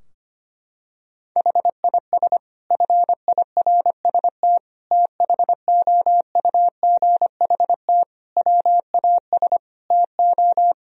Morse code, heard as a keyed tone, about 25 wpm, 700 Hz. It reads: HIS FIRST THOUGHT WAS TO
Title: Sentences from Top 1000 Words - Review Part 3 - 25wpm